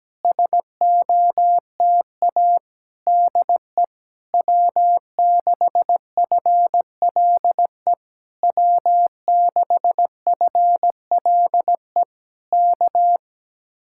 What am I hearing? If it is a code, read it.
Morse code, heard as a keyed tone, about 17 wpm, 700 Hz. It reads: SOTA DE W6FLE W6FLE K
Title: Speed-Racing Common QSO Phrases - 17wpm